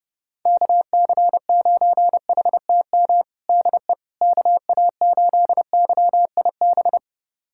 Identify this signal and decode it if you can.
Morse code, heard as a keyed tone, about 30 wpm, 700 Hz. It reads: KC9HTM DE KA8YIB
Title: USA Callsign DE Callsign 30wpm